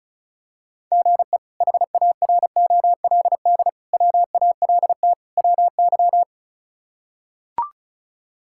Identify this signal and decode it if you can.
Morse code, heard as a keyed tone, about 34 wpm, 700 Hz. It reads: GE HAROLD WALT WY
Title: K1USN SST QSO Practice - 35wpm